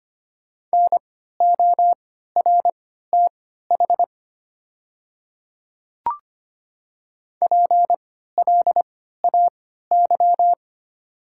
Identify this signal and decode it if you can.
Morse code, heard as a keyed tone, about 25 wpm, spaced as for 14 wpm, 690 Hz. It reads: NORTH PLAY
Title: 3x Character Spacing - Top 300 Words - 25wpm